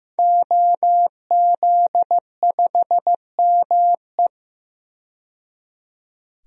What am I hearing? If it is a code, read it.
Morse code, heard as a keyed tone, about 15 wpm, 700 Hz. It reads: OZ5ME
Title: DXCC-100 Call Signs - 5 Characters - 15wpm